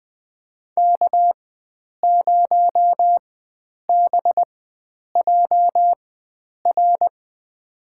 Morse code, heard as a keyed tone, about 21 wpm, spaced as for 9 wpm, 700 Hz. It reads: K0BJR
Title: USA Call Signs - 4x Character Spacing - 20wpm